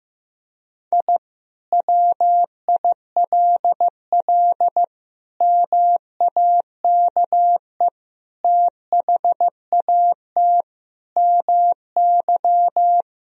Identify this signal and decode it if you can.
Morse code, heard as a keyed tone, about 15 wpm, 700 Hz. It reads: I WILL MAKE THAT MY
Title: Sentences from Top 1000 Words - Review Part 3 - 15wpm